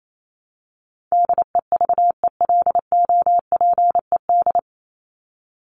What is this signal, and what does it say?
Morse code, heard as a keyed tone, about 28 wpm, 700 Hz. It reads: DEVELOPED
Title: ED Suffix - Top 100 Words with Base Words 28wpm